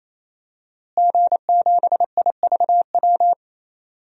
Morse code, heard as a keyed tone, about 28 wpm, 700 Hz. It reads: G7IVW
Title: DXCC-100 Call Signs - 5 Characters - Repeated 3x - 28wpm